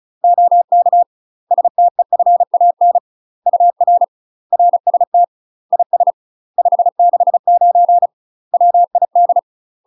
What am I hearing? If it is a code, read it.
Morse code, heard as a keyed tone, about 34 wpm, 700 Hz. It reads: OK STEFAN UR RST IS 569 WID
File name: UK MASTER QSOs TRAINING / 35WPM / UK MASTER QSOs 0008